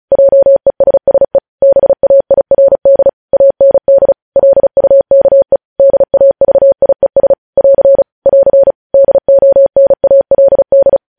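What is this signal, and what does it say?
Morse code, heard as a keyed tone, about 36 wpm, 550 Hz. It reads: JESSE BAIRD AND LUKE DAVIES <AR> <AR> DONALD
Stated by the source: Morse news headlines practice at 35 words per minute.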